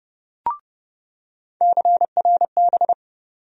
Morse code, heard as a keyed tone, about 30 wpm, 700 Hz. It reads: CRB
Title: USA Call Sign Postfixes - 3 Letters 30wpm